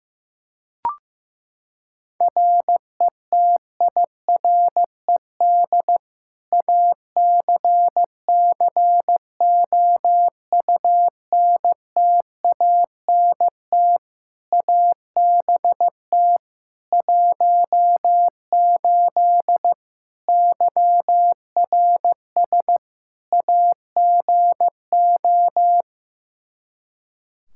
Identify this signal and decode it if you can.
Morse code, heard as a keyed tone, about 15 wpm, 700 Hz. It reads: RETIRED ACCOUNTANT ABT 18 YRS AGO
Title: K1USN SST QSO Practice 15wpm